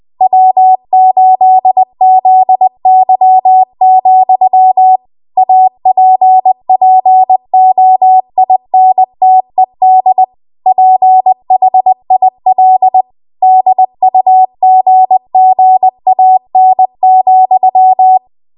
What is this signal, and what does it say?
Morse code, heard as a keyed tone, about 20 wpm, 750 Hz. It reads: W8ZY, APPOINTED PHIL DUGGAN,
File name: Archive / 20 WPM / 20WPM